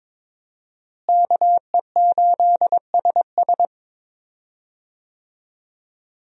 Morse code, heard as a keyed tone, about 22 wpm, 700 Hz. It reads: KE8SS
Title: USA Call Signs - 5 Characters - Repeated 2x - 22wpm